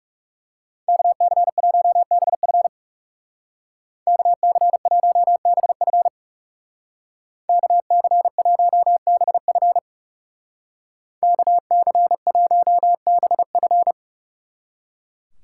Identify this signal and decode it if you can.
Morse code, heard as a keyed tone, about 36 wpm, 700 Hz. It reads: KC1BF KC1BF KC1BF KC1BF
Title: Speed Racing - USA 2x2 Call Signs 30wpm